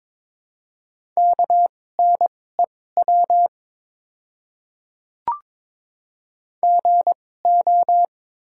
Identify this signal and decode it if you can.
Morse code, heard as a keyed tone, about 22 wpm, spaced as for 16 wpm, 700 Hz. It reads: KNEW GO
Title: 2x Character Spacing - Top 500 Words - 22wpm